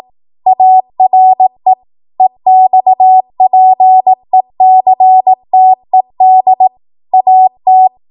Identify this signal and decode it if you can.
Morse code, heard as a keyed tone, about 18 wpm, 750 Hz. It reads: ARE EXPECTED AT
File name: Morse / 18WPM